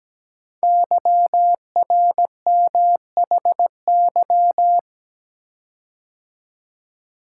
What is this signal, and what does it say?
Morse code, heard as a keyed tone, about 17 wpm, 700 Hz. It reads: YRMHY
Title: Code Groups - 5 Letters - 17wpm